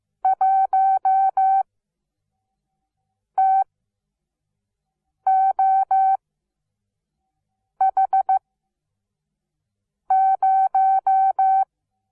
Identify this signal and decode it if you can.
Morse code, heard as a keyed tone, about 15 wpm, spaced as for 5 wpm, 770 Hz. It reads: 1TOH0
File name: Onlinelehrgang / morsen / DJ4UF Morsen17